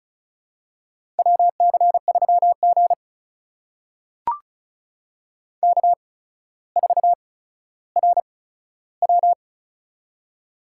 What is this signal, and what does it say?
Morse code, heard as a keyed tone, about 36 wpm, spaced as for 15 wpm, 700 Hz. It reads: WC3G K4RW
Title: USA Call Signs - 4 Characters - 8x Character Spacing - 35wpm